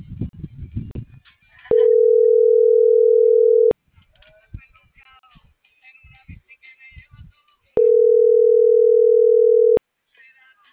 Ringback tone (North America), 440 + 480 Hz.